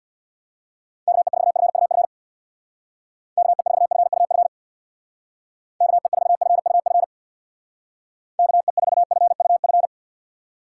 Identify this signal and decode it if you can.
Morse code, heard as a keyed tone, about 61 wpm, 700 Hz. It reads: KE4FUF KE4FUF KE4FUF KE4FUF
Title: Speed Racing - USA Call Signs 50wpm